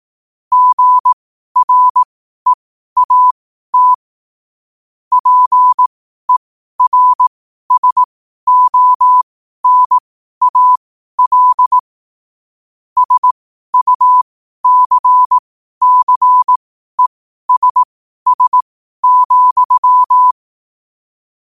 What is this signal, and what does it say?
Morse code, heard as a keyed tone, about 18 wpm, spaced as for 12 wpm, 1000 Hz. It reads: GREAT PERSONAL SUCCESS,